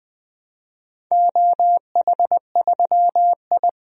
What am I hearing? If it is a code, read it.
Morse code, heard as a keyed tone, about 21 wpm, 700 Hz. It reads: OH3I
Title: DXCC-100 Call Signs - 4 Characters - 20wpm